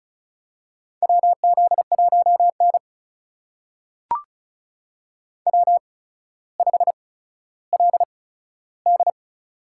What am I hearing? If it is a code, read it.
Morse code, heard as a keyed tone, about 36 wpm, spaced as for 15 wpm, 700 Hz. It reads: WZ1N W5LD
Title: USA Call Signs - 4 Characters - 8x Character Spacing - 35wpm